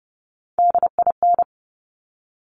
Morse code, heard as a keyed tone, about 30 wpm, 700 Hz. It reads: DIN
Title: Top Three Letter Combinations in Words 30wpm